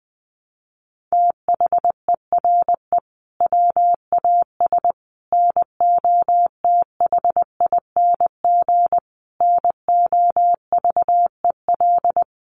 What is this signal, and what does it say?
Morse code, heard as a keyed tone, about 19 wpm, 700 Hz. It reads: THERE WAS NOTHING NOVEL
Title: Sentences from Top 1300 Words 20wpm